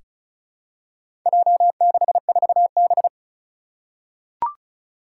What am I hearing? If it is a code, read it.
Morse code, heard as a keyed tone, about 36 wpm, 700 Hz. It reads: J64B
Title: Four Challenging Letter-Number Combinations 35wpm